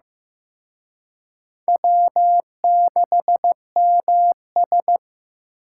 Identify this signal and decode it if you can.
Morse code, heard as a keyed tone, about 18 wpm, 700 Hz. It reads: W6MS
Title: Speed Racing - USA Call Signs 15wpm